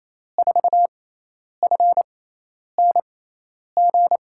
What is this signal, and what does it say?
Morse code, heard as a keyed tone, about 28 wpm, spaced as for 10 wpm, 700 Hz. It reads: 4FNG
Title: USA Call Signs - 6 Characters - 6x Character Spacing - 28wpm